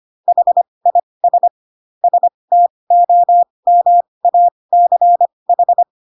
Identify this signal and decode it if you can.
Morse code, heard as a keyed tone, about 25 wpm, spaced as for 20 wpm, 700 Hz. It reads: HIS STOMACH
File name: Pinocchio / Pinocchio ch5 15 25 ik1hge